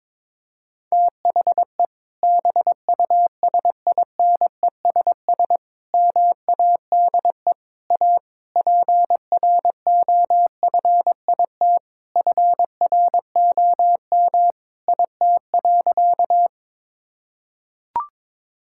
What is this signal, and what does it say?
Morse code, heard as a keyed tone, about 21 wpm, 700 Hz. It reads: THE BUSINESS MADE A PROFIT FROM IT.
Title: Sentences from Top 1000 Words - Review Part 5 - 22wpm